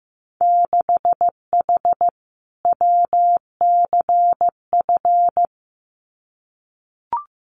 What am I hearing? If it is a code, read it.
Morse code, heard as a keyed tone, about 15 wpm, 700 Hz. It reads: 6H WCF
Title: Winter Field Day Exchanges Encore 15wpm